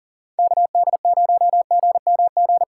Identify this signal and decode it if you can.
Morse code, heard as a keyed tone, about 40 wpm, 700 Hz. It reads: KD0GMG
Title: USA Call Signs - 6x Character Spacing - 40wpm